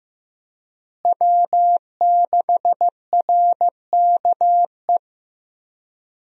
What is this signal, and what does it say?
Morse code, heard as a keyed tone, about 15 wpm, 700 Hz. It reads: W6RKE
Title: USA Call Signs - 5 Characters - Repeated 3x - 15wpm